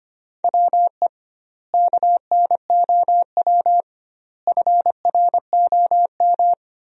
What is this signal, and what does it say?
Morse code, heard as a keyed tone, about 25 wpm, 700 Hz. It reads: WE KNOW FROM
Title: 2x Word-Spacing - Sentences from Top 500 Words - 25wpm